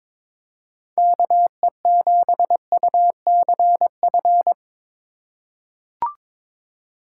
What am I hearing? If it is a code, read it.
Morse code, heard as a keyed tone, about 22 wpm, 700 Hz. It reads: KE7UCF
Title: USA Call Signs - 4x Character Spacing - 22wpm